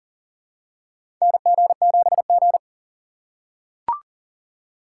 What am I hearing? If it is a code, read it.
Morse code, heard as a keyed tone, about 40 wpm, 700 Hz. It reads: NG7G
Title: USA Call Signs - 4 Characters - 2x Character Spacing - 40wpm